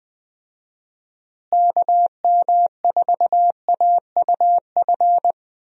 Morse code, heard as a keyed tone, about 20 wpm, 700 Hz. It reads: KM4AUF
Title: USA Call Signs - 6 Characters - Repeated 3x - 20wpm